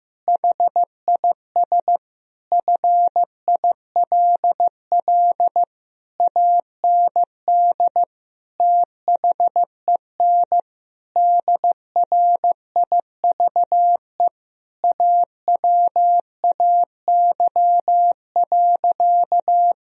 Morse code, heard as a keyed tone, about 15 wpm, 700 Hz. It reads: HIS FILL AND THEN DRIVE AWAY.
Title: Sentences from Top 1000 Words - Review Part 3 - 15wpm